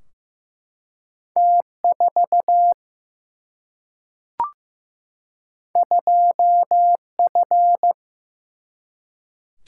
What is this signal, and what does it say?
Morse code, heard as a keyed tone, about 15 wpm, 700 Hz. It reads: T4 2F
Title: Single-Letter Single-Number 15wpm